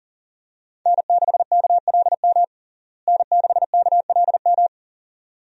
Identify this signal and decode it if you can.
Morse code, heard as a keyed tone, about 40 wpm, 700 Hz. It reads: N6KLM N6KLM
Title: USA Call Signs - 5 Characters - Repeated 2x - 40wpm